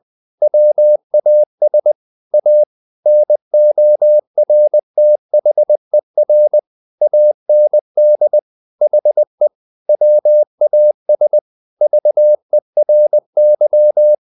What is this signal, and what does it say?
Morse code, heard as a keyed tone, about 20 wpm, 600 Hz. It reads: WAS A NORTHER AND HE WAS VERY
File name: the old man and the sea part 2 20wpm